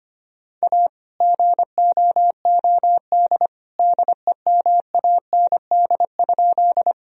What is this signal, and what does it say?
Morse code, heard as a keyed tone, about 25 wpm, 700 Hz. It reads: A GOOD DEMAND?
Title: Sentences from Top 1000 Words - Review Part 4 - 25wpm